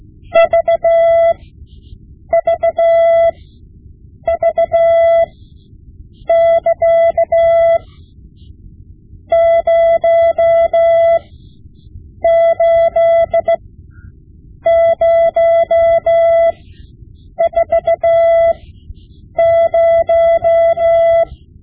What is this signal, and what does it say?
Morse code, heard as a keyed tone, about 16 wpm, spaced as for 7 wpm, 670 Hz. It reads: VVV<KA> 08040
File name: Morse Lessons / Morse Lessons / Tape-5B